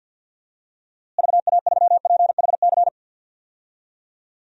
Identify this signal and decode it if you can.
Morse code, heard as a keyed tone, about 51 wpm, 700 Hz. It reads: VA3PHC
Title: DXCC-100 Call Signs - 6 Characters - Repeated 3x - 50wpm